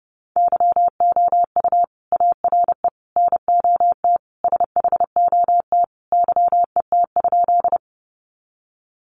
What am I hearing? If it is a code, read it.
Morse code, heard as a keyed tone, about 30 wpm, 700 Hz. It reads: YOU ARE NOT SHOT YET?
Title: Sentences from Top 1100 Words 30wpm